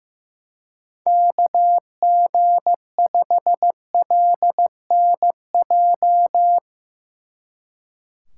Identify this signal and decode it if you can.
Morse code, heard as a keyed tone, about 16 wpm, 700 Hz. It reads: KG5LNJ
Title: Speed Racing - USA Call Signs 15wpm